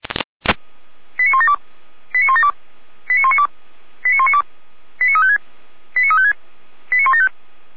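Five-tone selective call (ZVEI), heard as quick runs of five tones, tones of about 70 ms. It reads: 89172 89172 89192 89192 89356 89356 89175
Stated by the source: ZVEI five-tone call 89108 received by SDR.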